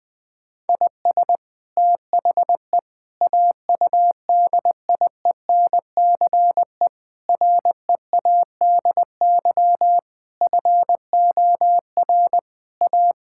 Morse code, heard as a keyed tone, about 21 wpm, 700 Hz. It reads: IS THE AUDIENCE READY FOR A
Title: Sentences from Top 1000 Words - Review Part 5 - 20wpm